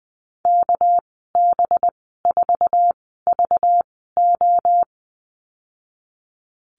Morse code, heard as a keyed tone, about 20 wpm, spaced as for 14 wpm, 700 Hz. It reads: KB4VO
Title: 245 Lesson 39g - Letter z - Call Signs 20wpm